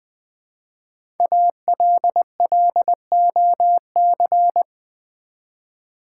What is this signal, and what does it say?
Morse code, heard as a keyed tone, about 20 wpm, 700 Hz. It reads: ALLOC
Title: Build A Word - 5 Letters At A Time - 10-Letter Words - 20wpm